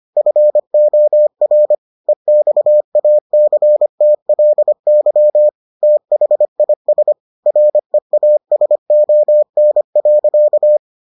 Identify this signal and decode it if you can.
Morse code, heard as a keyed tone, about 25 wpm, 600 Hz. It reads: FOR EXACTLY THIS REASON.